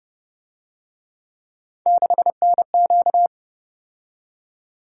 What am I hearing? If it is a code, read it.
Morse code, heard as a keyed tone, about 30 wpm, 700 Hz. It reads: 6NQ
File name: Speed-Racing - Three Letter-Number Combinations - 25wpm